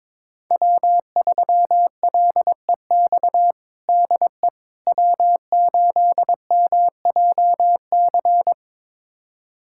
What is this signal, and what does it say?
Morse code, heard as a keyed tone, about 22 wpm, 700 Hz. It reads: W3LEX DE W8MJC
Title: USA Callsign DE Callsign - 5 Characters - 22wpm